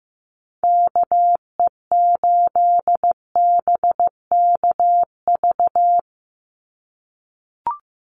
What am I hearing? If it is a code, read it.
Morse code, heard as a keyed tone, about 15 wpm, 700 Hz. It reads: KE8BKV
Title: USA Call Signs 15wpm